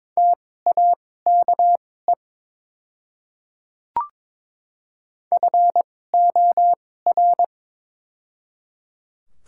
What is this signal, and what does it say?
Morse code, heard as a keyed tone, about 22 wpm, spaced as for 16 wpm, 700 Hz. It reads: TAKE FOR
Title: 2x Character Spacing - Top 200 Words - 22wpm